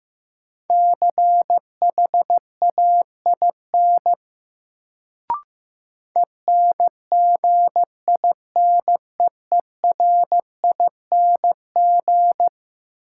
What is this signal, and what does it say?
Morse code, heard as a keyed tone, about 15 wpm, 700 Hz. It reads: CHAIN ENGINEERING
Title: Sentences from Top 1500 Words - New Word Drill - 15wpm